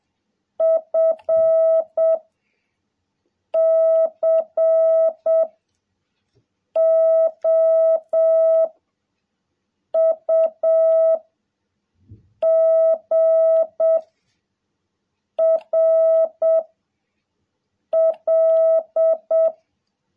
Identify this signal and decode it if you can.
Morse code, heard as a keyed tone, about 7 wpm, 660 Hz. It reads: F C O U G R L
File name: jh8cbh / 2021cw